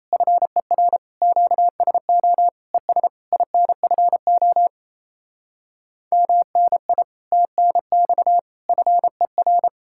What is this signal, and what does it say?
Morse code, heard as a keyed tone, about 38 wpm, 700 Hz. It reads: FER QSO ES INFO MNI TNX FER